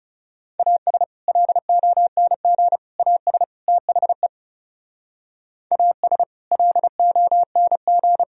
Morse code, heard as a keyed tone, about 40 wpm, 700 Hz. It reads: AS LONG AS THE AS LONG